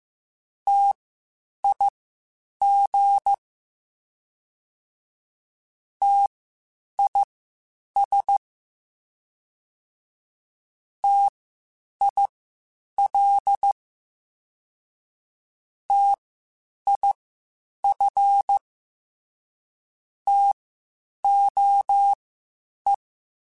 Morse code, heard as a keyed tone, about 15 wpm, spaced as for 8 wpm, 800 Hz. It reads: TIG TIS TIL TIF TOE